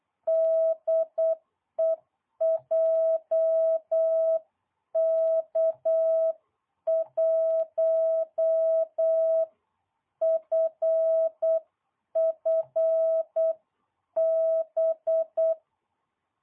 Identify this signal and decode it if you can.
Morse code, heard as a keyed tone, about 8 wpm, 660 Hz. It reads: DE JK1FFB